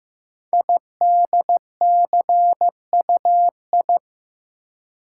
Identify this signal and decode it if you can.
Morse code, heard as a keyed tone, about 15 wpm, 700 Hz. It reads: IDCUI
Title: Code Groups - 5 Letter:Numbers - 15wpm